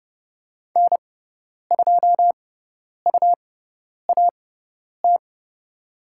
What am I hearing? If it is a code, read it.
Morse code, heard as a keyed tone, about 30 wpm, spaced as for 10 wpm, 700 Hz. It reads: N2UAT
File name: Lesson 17g - Letter u - Call Signs 30wpm